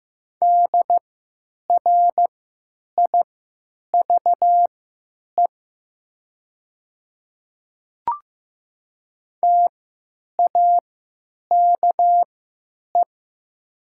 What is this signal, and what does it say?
Morse code, heard as a keyed tone, about 15 wpm, spaced as for 9 wpm, 700 Hz. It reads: DRIVE TAKE
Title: 3x Character Spacing - Top 500 Words - 15wpm